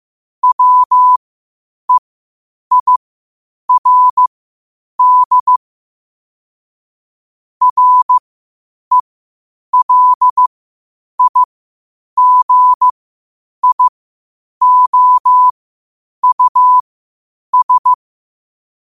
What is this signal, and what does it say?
Morse code, heard as a keyed tone, about 15 wpm, spaced as for 8 wpm, 1000 Hz. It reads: WEIRD RELIGIOUS
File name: qotdfiles / 8wpm